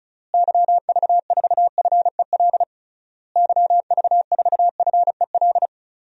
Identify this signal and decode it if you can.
Morse code, heard as a keyed tone, about 36 wpm, 700 Hz. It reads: YV4FEL YV4FEL
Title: DXCC-100 Call Signs - 6 Characters - Repeated 3x - 35wpm